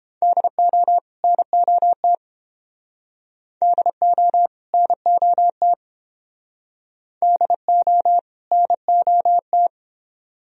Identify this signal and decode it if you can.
Morse code, heard as a keyed tone, about 28 wpm, 700 Hz. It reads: DO NOT DO NOT DO NOT